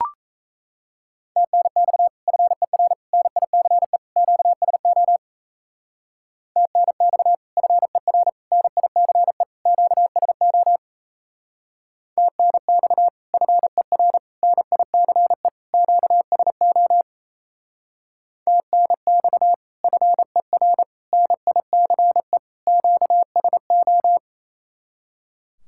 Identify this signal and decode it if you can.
Morse code, heard as a keyed tone, about 36 wpm, 700 Hz. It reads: TNX FER NICE QSO TNX FER NICE QSO TNX FER NICE QSO TNX FER NICE QSO